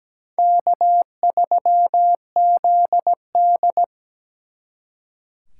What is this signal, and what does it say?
Morse code, heard as a keyed tone, about 20 wpm, 700 Hz. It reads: K3ZD
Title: Speed Racing - USA 1x2 Call Signs 17wpm